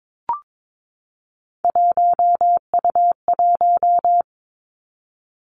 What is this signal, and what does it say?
Morse code, heard as a keyed tone, about 22 wpm, 700 Hz. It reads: 1U1
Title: Three Challenging Letter-Number Combinations 22wpm